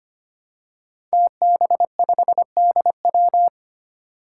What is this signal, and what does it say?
Morse code, heard as a keyed tone, about 25 wpm, 700 Hz. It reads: TB5DW
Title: Five Challenging Letter-Number Combinations 25wpm